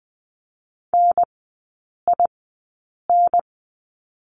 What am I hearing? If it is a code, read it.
Morse code, heard as a keyed tone, about 20 wpm, spaced as for 8 wpm, 700 Hz. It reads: NIN
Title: 093 Lesson 16e - Letter c - Three Characters 20wpm